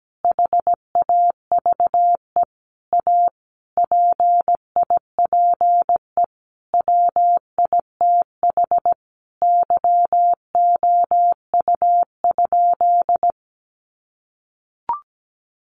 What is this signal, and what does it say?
Morse code, heard as a keyed tone, about 17 wpm, 700 Hz. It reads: HAVE A PIPE WITH YOU?
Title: Sentences from Top 1500 Words 17wpm